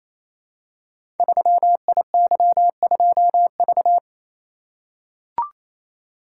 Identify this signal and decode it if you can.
Morse code, heard as a keyed tone, about 29 wpm, 700 Hz. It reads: 3IY2V
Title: Code Groups - 5 Letter:Numbers - 28wpm